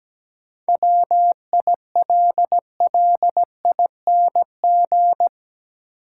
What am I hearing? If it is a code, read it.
Morse code, heard as a keyed tone, about 17 wpm, 700 Hz. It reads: WILLING
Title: Sentences from Top 1000 Words - New Word Drill - 17wpm